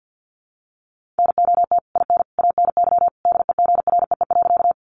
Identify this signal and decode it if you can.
Morse code, heard as a keyed tone, about 50 wpm, 700 Hz. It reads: NOT IN ANY DEGREE.
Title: Sentences from Top 1500 Words 50wpm